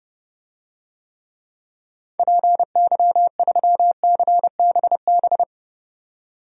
Morse code, heard as a keyed tone, about 30 wpm, 700 Hz. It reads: PY3CBB
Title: DXCC-100 Call Signs - 6 Characters - 30wpm